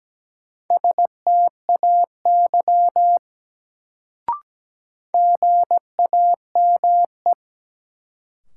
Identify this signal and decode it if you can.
Morse code, heard as a keyed tone, about 17 wpm, 700 Hz. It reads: STAY GAME
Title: Sentences from Top 500 Words - New Word Drill - 17wpm